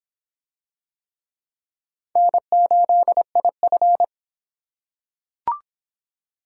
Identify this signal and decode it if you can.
Morse code, heard as a keyed tone, about 26 wpm, 700 Hz. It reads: N8IF